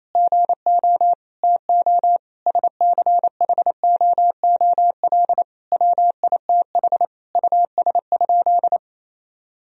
Morse code, heard as a keyed tone, about 28 wpm, 700 Hz. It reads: GO TO SCHOOL WITH US?
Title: Sentences from Top 1000 Words - Review Part 3 - 28wpm